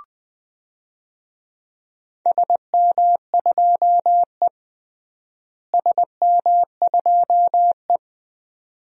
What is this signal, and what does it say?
Morse code, heard as a keyed tone, about 21 wpm, 700 Hz. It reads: SM2E SM2E
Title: DXCC-100 Call Signs - 4 Characters - Repeated 2x - 20wpm